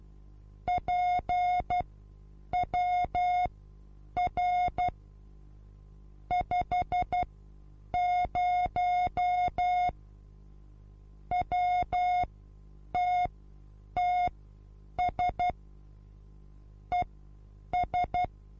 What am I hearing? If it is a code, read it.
Morse code, heard as a keyed tone, about 12 wpm, spaced as for 8 wpm, 720 Hz. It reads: PWR 50 WTTS ES
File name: Onlinelehrgang / morsen / DJ4UF Morsen23